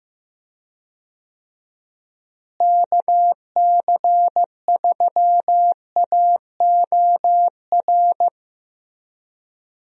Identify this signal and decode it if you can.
Morse code, heard as a keyed tone, about 15 wpm, 700 Hz. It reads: KC3AOR
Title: USA Call Signs - 6 Characters - Repeated 2x - 15wpm